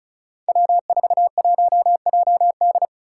Morse code, heard as a keyed tone, about 36 wpm, 700 Hz. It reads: W41JD
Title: Five Challenging Letter-Number Combinations 35wpm